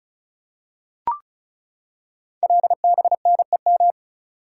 Morse code, heard as a keyed tone, about 36 wpm, 690 Hz. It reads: LBNEM
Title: Code Groups - 5 Letters - 35wpm